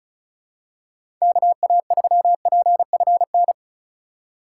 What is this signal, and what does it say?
Morse code, heard as a keyed tone, about 36 wpm, 700 Hz. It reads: KA3PFN
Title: USA Call Signs - 8x Character Spacing - 35wpm